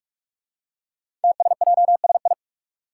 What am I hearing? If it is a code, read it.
Morse code, heard as a keyed tone, about 44 wpm, 700 Hz. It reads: TSJSI
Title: Five Challenging Letter-Number Combinations 45wpm